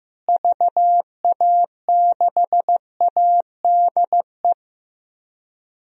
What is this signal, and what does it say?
Morse code, heard as a keyed tone, about 15 wpm, 700 Hz. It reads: VA6ADE
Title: DXCC-100 Call Signs - 6 Characters - Repeated 2x - 15wpm